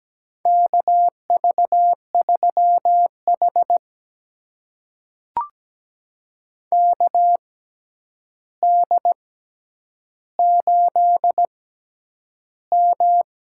Morse code, heard as a keyed tone, about 17 wpm, spaced as for 9 wpm, 700 Hz. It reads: KV3H KD8M
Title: USA Call Signs - 4 Characters - 6x Character Spacing - 17wpm